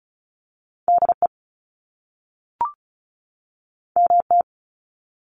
Morse code, heard as a keyed tone, about 35 wpm, 700 Hz. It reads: DE MT
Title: US States 35wpm